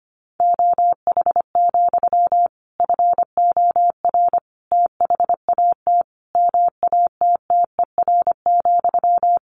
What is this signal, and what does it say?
Morse code, heard as a keyed tone, about 25 wpm, 700 Hz. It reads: OH, FOR THAT MATTER,
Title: Sentences from Top 1400 Words 25wpm